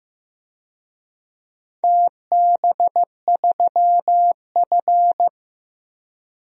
Morse code, heard as a keyed tone, about 15 wpm, 700 Hz. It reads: TB3F
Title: DXCC-100 Call Signs - 4 Characters - Repeated 2x - 15wpm